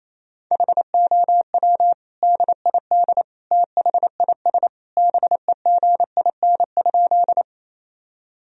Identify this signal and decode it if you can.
Morse code, heard as a keyed tone, about 28 wpm, 700 Hz. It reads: HOW DID THIS BEGIN?
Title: Sentences from Top 1000 Words - Review Part 1 - 28wpm